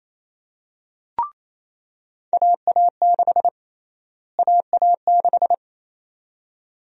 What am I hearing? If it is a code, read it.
Morse code, heard as a keyed tone, about 28 wpm, 700 Hz. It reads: AA6 AA6
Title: USA Call Sign Prefixes - Repeated 2x - 28wpm